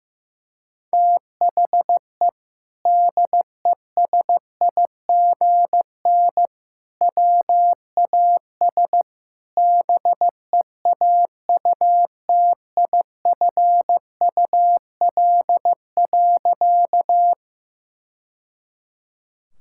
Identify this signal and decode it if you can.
Morse code, heard as a keyed tone, about 15 wpm, 700 Hz. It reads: THE DESIGN WAS BEAUTIFUL.
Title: Sentences from Top 1000 Words - Review Part 5 - 15wpm